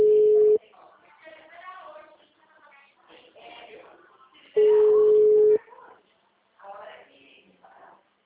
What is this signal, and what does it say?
Ringback tone (Europe, 425 Hz), with speech.
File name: out- -2005- - - .7516